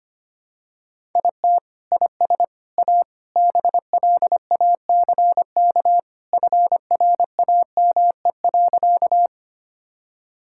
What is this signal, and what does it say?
Morse code, heard as a keyed tone, about 25 wpm, 700 Hz. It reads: IT IS A BLACK FRAME.
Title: Sentences from Top 1000 Words - Review Part 6 - 25wpm